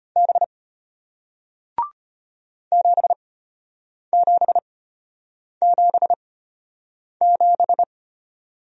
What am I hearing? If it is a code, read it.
Morse code, heard as a keyed tone, about 33 wpm, spaced as for 8 wpm, 700 Hz. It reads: B 7777